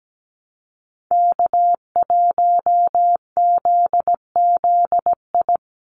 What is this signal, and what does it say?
Morse code, heard as a keyed tone, about 17 wpm, 700 Hz. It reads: K1ZZI
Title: Call Signs - 17wpm